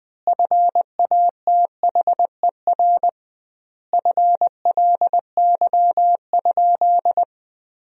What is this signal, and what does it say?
Morse code, heard as a keyed tone, about 21 wpm, 700 Hz. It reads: FATHER FLY?
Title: 2x Word-Spacing - Sentences from Top 500 Words - 20wpm